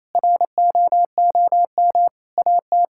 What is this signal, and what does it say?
Morse code, heard as a keyed tone, about 28 wpm, 700 Hz. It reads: ROOM AT
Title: Sentences from Top 1600 Words 28wpm